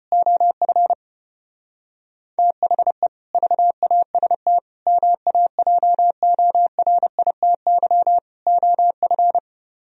Morse code, heard as a keyed tone, about 33 wpm, 700 Hz. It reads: OF THE VAST MAJORITY OF